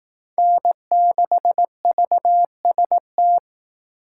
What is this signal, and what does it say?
Morse code, heard as a keyed tone, about 18 wpm, 700 Hz. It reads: N6VST
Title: Speed Racing - USA 1x3 Call Signs 15wpm